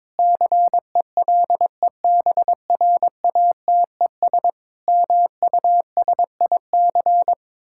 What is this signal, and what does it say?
Morse code, heard as a keyed tone, about 22 wpm, 700 Hz. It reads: CELEBRATES MUSIC
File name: Sentences with 7-Word Phrases - 22wpm